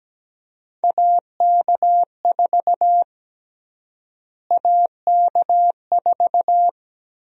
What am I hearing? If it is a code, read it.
Morse code, heard as a keyed tone, about 17 wpm, 700 Hz. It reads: AK4 AK4
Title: USA Call Sign Prefixes - Repeated 2x - 17wpm